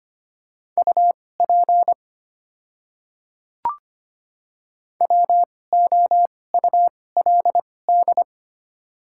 Morse code, heard as a keyed tone, about 25 wpm, spaced as for 18 wpm, 700 Hz. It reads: UP WOULD
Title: 2x Character Spacing - Top 100 Words - 25wpm